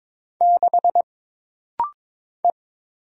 Morse code, heard as a keyed tone, about 23 wpm, 700 Hz. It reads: 6 E
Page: This rapid-fire timing is geared to push your ICR (Instant Character Recognition) skills.